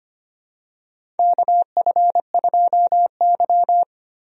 Morse code, heard as a keyed tone, about 25 wpm, 700 Hz. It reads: KF2Y
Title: USA Call Signs - 4 Characters - Repeated 2x - 25wpm